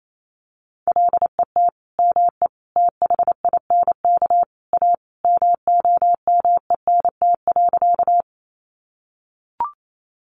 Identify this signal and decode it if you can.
Morse code, heard as a keyed tone, about 28 wpm, 700 Hz. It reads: LET ME THINK A MOMENT.
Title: Sentences from Top 1000 Words 28wpm